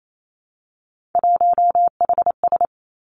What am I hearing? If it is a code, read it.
Morse code, heard as a keyed tone, about 28 wpm, 700 Hz. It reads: 1HS
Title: Three Challenging Letter-Number Combinations 28wpm